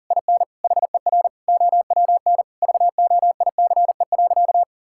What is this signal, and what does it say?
Morse code, heard as a keyed tone, about 40 wpm, 700 Hz. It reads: IN HER OWN VOICE.
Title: Sentences from Top 1600 Words 40wpm